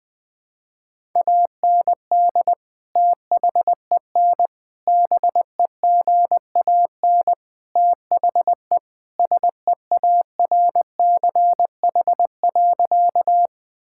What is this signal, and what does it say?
Morse code, heard as a keyed tone, about 20 wpm, 700 Hz. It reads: AND THEN BEGAN THE SEARCH.
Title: Sentences from Top 1000 Words - Review Part 5 - 20wpm